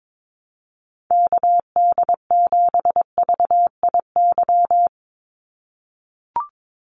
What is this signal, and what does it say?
Morse code, heard as a keyed tone, about 22 wpm, 700 Hz. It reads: KD7VIY
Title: USA Call Signs 22wpm